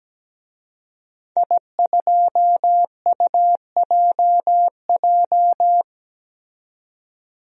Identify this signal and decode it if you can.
Morse code, heard as a keyed tone, about 17 wpm, 700 Hz. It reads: I2UJJ
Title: DXCC-100 Call Signs - 5 Characters - 17wpm